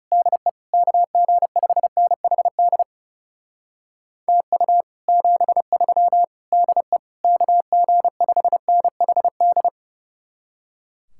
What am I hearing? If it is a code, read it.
Morse code, heard as a keyed tone, about 37 wpm, 700 Hz. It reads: DE KG5NHD TU 73 DE KG5NHD
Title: Speed-Racing Common QSO Phrases - 30wpm